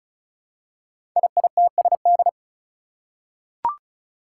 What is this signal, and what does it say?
Morse code, heard as a keyed tone, about 36 wpm, 700 Hz. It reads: IITSD
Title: Code Groups - 5 Letters - 35wpm